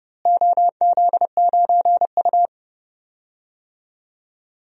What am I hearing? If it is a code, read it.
Morse code, heard as a keyed tone, about 31 wpm, 700 Hz. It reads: OZ9U
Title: DXCC-100 Call Signs - 4 Characters - Repeated 2x - 30wpm